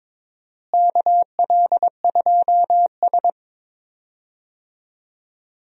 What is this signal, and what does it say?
Morse code, heard as a keyed tone, about 21 wpm, 700 Hz. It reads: KL2S
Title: USA Call Signs - 4 Characters - Repeated 2x - 22wpm